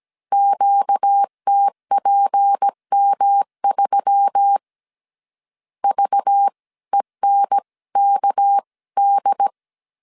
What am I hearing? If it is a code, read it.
Morse code, heard as a keyed tone, about 17 wpm, spaced as for 15 wpm, 800 Hz. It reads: QTPM3 VENKD
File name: 16wpm